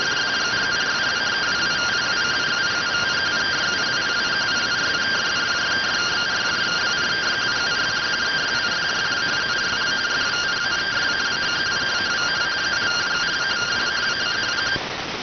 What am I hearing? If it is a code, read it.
Two-tone teleprinter signal, about 170 Hz shift.